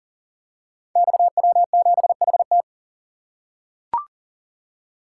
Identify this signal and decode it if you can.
Morse code, heard as a keyed tone, about 39 wpm, 700 Hz. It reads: XW7HT
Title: Code Groups - 5 Letter:Numbers - 40wpm